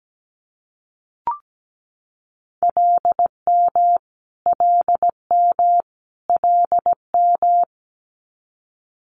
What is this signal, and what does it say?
Morse code, heard as a keyed tone, about 17 wpm, 700 Hz. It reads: LM LM LM
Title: Multi-letter Phonemes Repeated 3x 17wpm